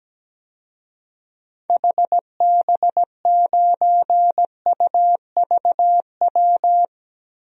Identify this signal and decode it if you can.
Morse code, heard as a keyed tone, about 17 wpm, 700 Hz. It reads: HB9UVW
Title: DXCC-100 Call Signs - 6 Characters - Repeated 2x - 17wpm